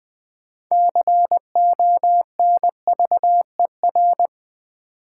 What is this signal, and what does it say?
Morse code, heard as a keyed tone, about 20 wpm, 700 Hz. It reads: CONVER
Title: Build A Word - 6 Letters At A Time - 12-Letter Words - 20wpm